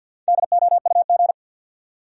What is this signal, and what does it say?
Morse code, heard as a keyed tone, about 52 wpm, 700 Hz. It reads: DOUG
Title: USA Common Names - 50wpm